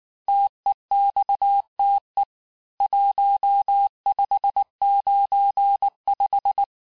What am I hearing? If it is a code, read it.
Morse code, heard as a keyed tone, about 19 wpm, 800 Hz. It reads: TEXTE 1595
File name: cours-du-lundi files / cours-1595-teste-CHIFFRE-vitesse18-MPM